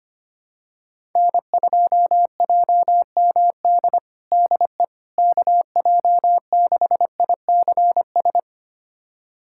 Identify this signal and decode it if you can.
Morse code, heard as a keyed tone, about 25 wpm, 700 Hz. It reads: N2JMD DE KJ6ICS
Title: USA Callsign DE Callsign 25wpm